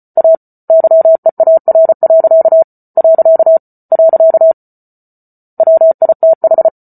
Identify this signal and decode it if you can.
Morse code, heard as a keyed tone, about 34 wpm, 650 Hz. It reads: A YEAR. . WITH
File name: Book Two / 08-Chapter 8 - 34 WPM